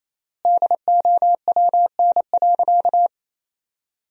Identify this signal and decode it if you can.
Morse code, heard as a keyed tone, about 28 wpm, 700 Hz. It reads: DOWN.
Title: 3x Word-Spacing - Sentences from Top 500 Words - 28wpm